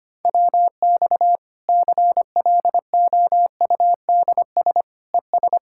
Morse code, heard as a keyed tone, about 25 wpm, 700 Hz. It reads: WX CLOUDS ES
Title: Practice QSO - Exchange 2 - Info - 25wpm